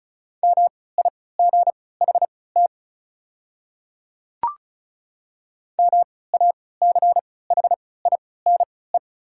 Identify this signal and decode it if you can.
Morse code, heard as a keyed tone, about 36 wpm, spaced as for 20 wpm, 700 Hz. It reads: MIGHT MACHINE
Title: 3x Character Spacing - Top 500 Words - 35wpm